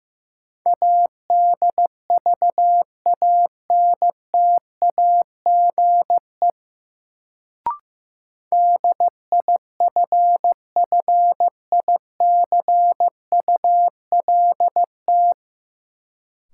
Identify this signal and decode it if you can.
Morse code, heard as a keyed tone, about 15 wpm, 700 Hz. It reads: ADVANTAGE DIFFICULT
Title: Sets of 4 Words - New Word Drill - 15wpm